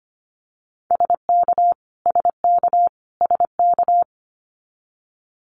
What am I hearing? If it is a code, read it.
Morse code, heard as a keyed tone, about 25 wpm, 700 Hz. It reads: SK SK SK
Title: Canadian Provinces and Territories - Repeated 3x - 25wpm